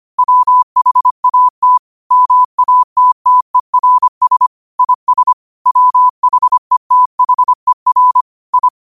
Morse code, heard as a keyed tone, about 25 wpm, 1000 Hz. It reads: WHAT MATTERS IS WHETHER I
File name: qotdfiles / 25wpm